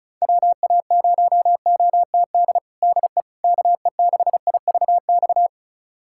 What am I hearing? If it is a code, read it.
Morse code, heard as a keyed tone, about 36 wpm, 700 Hz. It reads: WA0OTD DE KE6IVX
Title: USA Callsign DE Callsign 35wpm